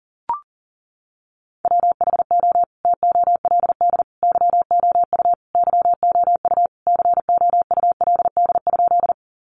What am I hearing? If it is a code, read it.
Morse code, heard as a keyed tone, about 40 wpm, 700 Hz. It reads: WHO TOLD YOU YOU COULD?
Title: Sentences from Top 400 Words 40wpm